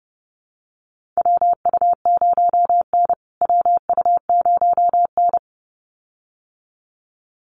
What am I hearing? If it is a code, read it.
Morse code, heard as a keyed tone, about 30 wpm, 700 Hz. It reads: WU0N WU0N
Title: USA Call Signs Repeated 2x 30wpm